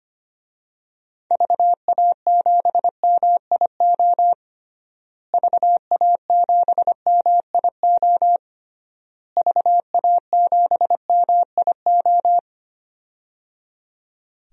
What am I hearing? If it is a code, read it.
Morse code, heard as a keyed tone, about 25 wpm, 700 Hz. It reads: VA7MIO VA7MIO VA7MIO
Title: DXCC-100 Call Signs - 6 Characters - Repeated 3x - 25wpm